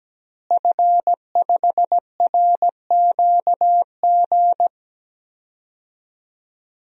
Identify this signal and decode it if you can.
Morse code, heard as a keyed tone, about 17 wpm, 700 Hz. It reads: F5RQG
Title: DXCC-100 Call Signs - 5 Characters - Repeated 3x - 17wpm